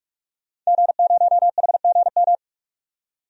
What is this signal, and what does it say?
Morse code, heard as a keyed tone, about 45 wpm, 700 Hz. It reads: G0HGM
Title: DXCC-100 Call Signs - 5 Characters - 45wpm